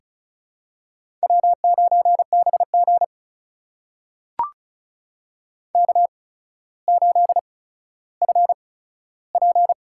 Morse code, heard as a keyed tone, about 36 wpm, spaced as for 15 wpm, 700 Hz. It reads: W9BG K8FP
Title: USA Call Signs - 4 Characters - 8x Character Spacing - 35wpm